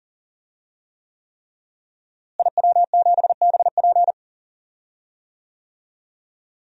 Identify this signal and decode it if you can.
Morse code, heard as a keyed tone, about 40 wpm, 700 Hz. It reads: IW7BP
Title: DXCC-100 Call Signs - 5 Characters - Repeated 3x - 40wpm